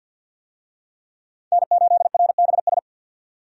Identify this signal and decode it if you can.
Morse code, heard as a keyed tone, about 50 wpm, 700 Hz. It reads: N8RBS
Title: USA Call Signs - 5 Characters - 50wpm